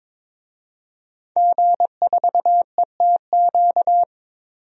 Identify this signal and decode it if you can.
Morse code, heard as a keyed tone, about 22 wpm, 700 Hz. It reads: G4ETQ
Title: DXCC-100 Call Signs - 5 Characters - 22wpm